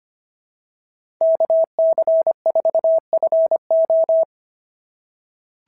Morse code, heard as a keyed tone, about 22 wpm, 650 Hz. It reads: KC4FO
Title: SOTA QSO Practice 25wpm